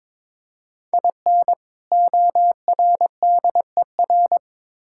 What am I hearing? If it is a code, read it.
Morse code, heard as a keyed tone, about 22 wpm, 700 Hz. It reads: IN ORDER
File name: practice / 22wpm / Build A Phrase - 2-Word Phrases - 22wpm